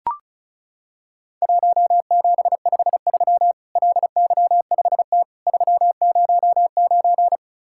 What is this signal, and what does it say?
Morse code, heard as a keyed tone, about 35 wpm, 700 Hz. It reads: 1753 LY5T 309
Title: Worked All Europe DX Contest - QTC Exchange - 35wpm